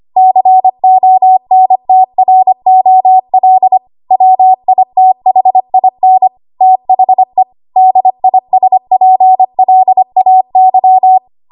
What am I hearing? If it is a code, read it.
Morse code, heard as a keyed tone, about 25 wpm, 750 Hz. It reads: CONTROL WITHIN THE DISPLAY
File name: Archive / 25 WPM / 25WPM